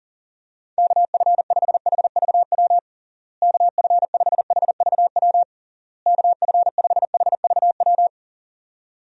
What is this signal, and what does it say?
Morse code, heard as a keyed tone, about 40 wpm, 700 Hz. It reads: KF5HVW KF5HVW KF5HVW
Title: USA Call Signs - 6 Characters - Repeated 3x - 40wpm